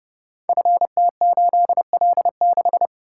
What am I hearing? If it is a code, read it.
Morse code, heard as a keyed tone, about 30 wpm, 700 Hz. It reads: FT8L6
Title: Code Groups - 5 Letter:Numbers - 30wpm